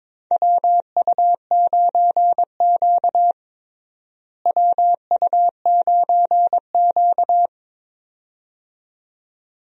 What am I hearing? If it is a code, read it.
Morse code, heard as a keyed tone, about 22 wpm, 700 Hz. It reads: WU9Q WU9Q
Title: USA Call Signs - 4 Characters - Repeated 2x - 22wpm